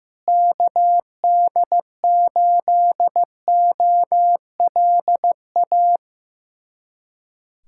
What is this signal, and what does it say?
Morse code, heard as a keyed tone, about 15 wpm, 700 Hz. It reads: KD8OLA
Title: USA Callsign DE Callsign - 6 Characters - 15wpm